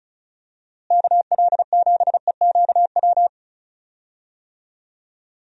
Morse code, heard as a keyed tone, about 36 wpm, 700 Hz. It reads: KL7EQW
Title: DXCC-100 Call Signs - 6 Characters - Repeated 3x - 35wpm